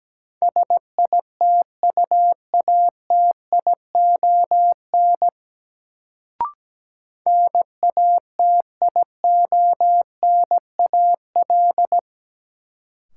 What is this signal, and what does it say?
Morse code, heard as a keyed tone, about 17 wpm, 700 Hz. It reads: SITUATION NATIONAL
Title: Sentences from Top 1200 Words - New Word Drill Alt - 17wpm